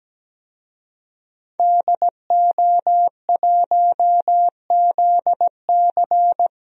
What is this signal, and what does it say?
Morse code, heard as a keyed tone, about 17 wpm, 700 Hz. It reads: DO1ZC
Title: DXCC-100 Call Signs - 5 Characters - Repeated 3x - 17wpm